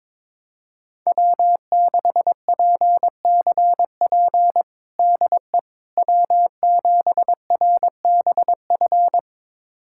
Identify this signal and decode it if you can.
Morse code, heard as a keyed tone, about 22 wpm, 700 Hz. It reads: W6PCP DE W7RBF
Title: USA Callsign DE Callsign - 5 Characters - 22wpm